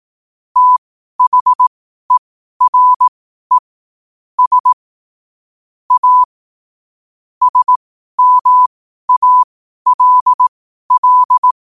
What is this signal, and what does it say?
Morse code, heard as a keyed tone, about 18 wpm, spaced as for 12 wpm, 1000 Hz. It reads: THERE S A SMALL
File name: qotdfiles / 12wpm